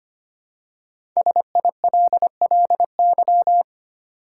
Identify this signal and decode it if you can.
Morse code, heard as a keyed tone, about 25 wpm, 700 Hz. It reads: SILLY
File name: 5 Letter Words - First 3 Letters Given - 25wpm